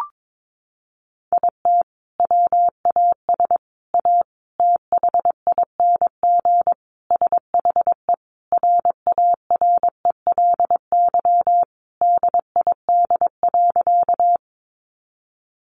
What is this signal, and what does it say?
Morse code, heard as a keyed tone, about 22 wpm, 700 Hz. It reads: IT WAS A THING SHE RARELY DID.
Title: Sentences from Top 1400 Words 22wpm